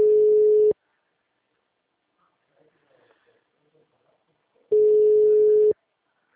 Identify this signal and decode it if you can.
Ringback tone (Europe, 425 Hz), with speech.